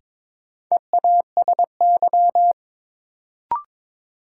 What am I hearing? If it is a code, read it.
Morse code, heard as a keyed tone, about 22 wpm, 700 Hz. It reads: EASY
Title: Sentences from Top 700 Words - New Word Drill - 22wpm